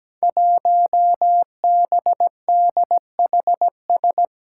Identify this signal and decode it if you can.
Morse code, heard as a keyed tone, about 17 wpm, 700 Hz. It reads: 1BDHS
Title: Five Challenging Letter-Number Combinations 17wpm